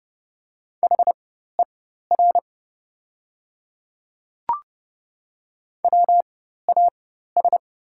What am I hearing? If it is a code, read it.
Morse code, heard as a keyed tone, about 30 wpm, spaced as for 14 wpm, 700 Hz. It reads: HER WAS
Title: 4x Character Spacing - Top 100 Words - 30wpm